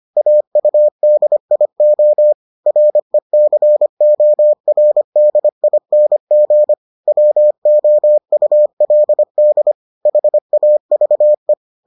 Morse code, heard as a keyed tone, about 25 wpm, 600 Hz. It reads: AUDIO RECORDING WOULD HAVE